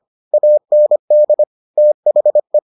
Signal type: Morse code

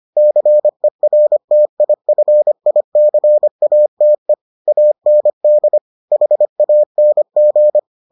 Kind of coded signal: Morse code

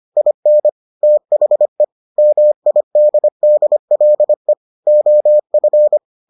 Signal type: Morse code

